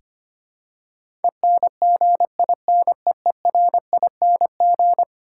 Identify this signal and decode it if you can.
Morse code, heard as a keyed tone, about 25 wpm, 700 Hz. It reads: ENGINEERING
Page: morse-code-ninja-practice-25wpm